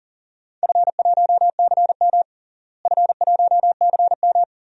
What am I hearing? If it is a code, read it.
Morse code, heard as a keyed tone, about 40 wpm, 700 Hz. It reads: F1CM F1CM
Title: DXCC-100 Call Signs - 4 Characters - Repeated 2x - 40wpm